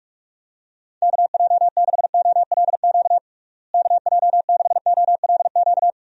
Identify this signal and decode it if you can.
Morse code, heard as a keyed tone, about 47 wpm, 700 Hz. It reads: KJ6OLQ KJ6OLQ
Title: USA Call Signs - 6 Characters - Repeated 2x - 45wpm